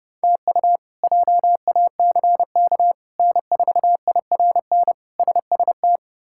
Morse code, heard as a keyed tone, about 32 wpm, 700 Hz. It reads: TU JACK N4IRN SST
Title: K1USN SST QSO Practice - 30wpm